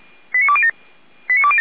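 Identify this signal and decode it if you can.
Five-tone selective call (ZVEI), heard as quick runs of five tones, tones of about 70 ms. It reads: 89207
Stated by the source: ZVEI five-tone selective call received with an SDR.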